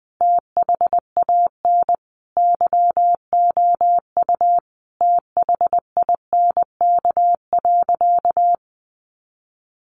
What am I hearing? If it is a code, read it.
Morse code, heard as a keyed tone, about 20 wpm, 700 Hz. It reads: THAN YOU THINK.
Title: Sentences from Top 400 Words 20wpm